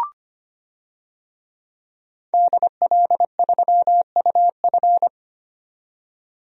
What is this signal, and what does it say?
Morse code, heard as a keyed tone, about 25 wpm, 700 Hz. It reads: DL3UF
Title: DXCC-100 Call Signs - 5 Characters - 25wpm